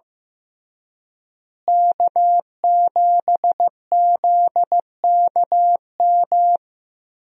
Morse code, heard as a keyed tone, about 15 wpm, 700 Hz. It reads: K7ZKM
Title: USA Call Signs - 5 Characters - Repeated 2x - 15wpm